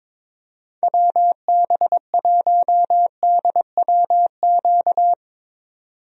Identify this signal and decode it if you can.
Morse code, heard as a keyed tone, about 22 wpm, 700 Hz. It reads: WB1DWQ
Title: USA Call Signs - 6 Characters - 22wpm